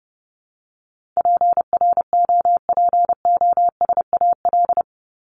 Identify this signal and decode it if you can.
Morse code, heard as a keyed tone, about 30 wpm, 700 Hz. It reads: PROPOSAL
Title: Top 1751-2000 Words 30wpm